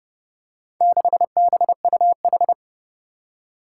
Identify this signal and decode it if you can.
Morse code, heard as a keyed tone, about 30 wpm, 700 Hz. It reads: 6BUH
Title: Four Challenging Letter-Number Combinations 30wpm